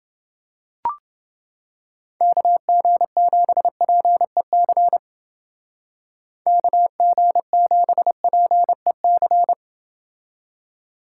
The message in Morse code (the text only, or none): KG7PEC KG7PEC